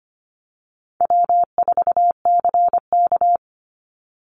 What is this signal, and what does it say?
Morse code, heard as a keyed tone, about 25 wpm, 700 Hz. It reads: W4CK
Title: Easy Call Signs - 25wpm